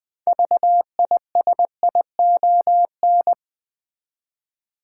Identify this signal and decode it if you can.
Morse code, heard as a keyed tone, about 20 wpm, 700 Hz. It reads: VISION
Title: 6 Letter Words - First 3 Letters Given - 20wpm